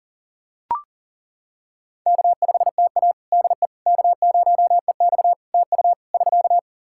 Morse code, heard as a keyed tone, about 38 wpm, 700 Hz. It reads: K5TA DE K0EX TU <SK>
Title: Practice QSO - Exchange 4 - Ending - 40wpm